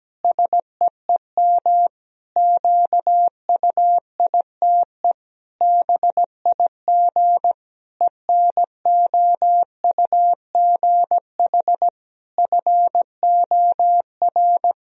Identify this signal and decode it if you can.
Morse code, heard as a keyed tone, about 17 wpm, 700 Hz. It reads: SEEM QUITE BIG ENOUGH FOR
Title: Sentences from Top 1000 Words - Review Part 2 - 17wpm